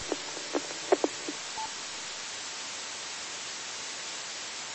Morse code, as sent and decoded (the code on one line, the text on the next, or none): none